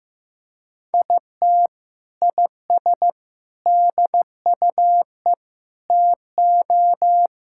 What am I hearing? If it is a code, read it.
Morse code, heard as a keyed tone, about 16 wpm, 700 Hz. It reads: IT IS DUE TO
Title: Sentences from Top 1000 Words - Review Part 6 - 15wpm